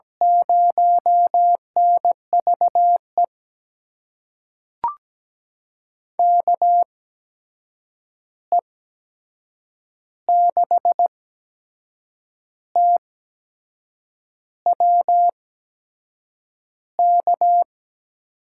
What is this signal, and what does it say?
Morse code, heard as a keyed tone, about 17 wpm, spaced as for 6 wpm, 700 Hz. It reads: 0NVE KE6TWK
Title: USA Call Signs - 6 Characters - 8x Character Spacing - 17wpm